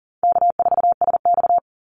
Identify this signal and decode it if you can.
Morse code, heard as a keyed tone, about 40 wpm, 700 Hz. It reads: K4SX
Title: Call Signs - 40wpm